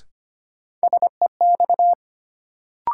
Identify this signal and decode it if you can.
Morse code, heard as a keyed tone, about 25 wpm, 700 Hz. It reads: SEX
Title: Sentences from Top 1600 Words - New Word Drill Alt - 25wpm